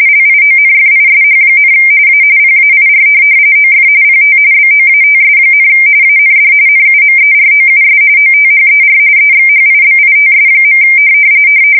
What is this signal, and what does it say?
RTTY, about 45.45 baud. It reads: LIONFISH, A PAIR OF PT BOATS, AND NUMEROUS OTHER EXHIBITS.
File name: fsk170 / USS-Massachusetts